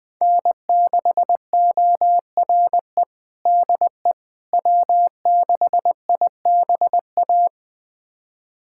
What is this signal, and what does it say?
Morse code, heard as a keyed tone, about 20 wpm, 700 Hz. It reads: N6ORE DE W6IBA
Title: USA Callsign DE Callsign - 5 Characters - 20wpm